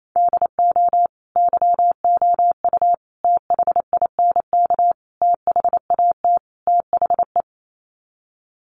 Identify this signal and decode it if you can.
Morse code, heard as a keyed tone, about 28 wpm, 700 Hz. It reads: DO YOU THINK THAT THE
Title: Sets of 5 Words Encore 28wpm